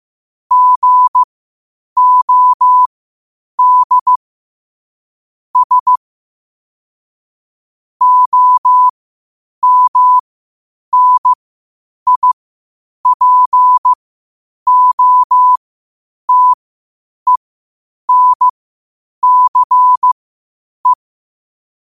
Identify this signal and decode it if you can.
Morse code, heard as a keyed tone, about 15 wpm, spaced as for 8 wpm, 1000 Hz. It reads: GOD S OMNIPOTENCE
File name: qotdfiles / 8wpm